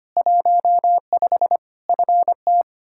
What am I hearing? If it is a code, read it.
Morse code, heard as a keyed tone, about 25 wpm, 700 Hz. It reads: 15 FT
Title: Practice QSO - Exchange 2 - Info - 25wpm